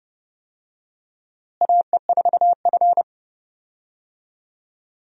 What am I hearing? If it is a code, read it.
Morse code, heard as a keyed tone, about 30 wpm, 700 Hz. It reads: AE4F
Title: USA Call Signs - 4 Characters - Repeated 3x - 30wpm